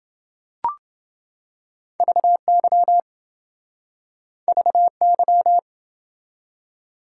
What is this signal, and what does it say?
Morse code, heard as a keyed tone, about 26 wpm, 700 Hz. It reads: VY VY